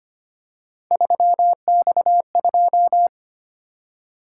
Morse code, heard as a keyed tone, about 25 wpm, 700 Hz. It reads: 3X2
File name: Three Letter-Number Combinations - 25wpm